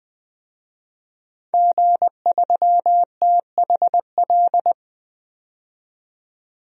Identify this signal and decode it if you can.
Morse code, heard as a keyed tone, about 20 wpm, 700 Hz. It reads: G3THL
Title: DXCC-100 Call Signs - 5 Characters - 20wpm